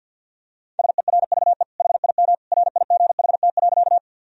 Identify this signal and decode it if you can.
Morse code, heard as a keyed tone, about 50 wpm, 700 Hz. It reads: SERVE HIM RIGHT.
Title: Sentences from Top 1000 Words - Review Part 2 - 50wpm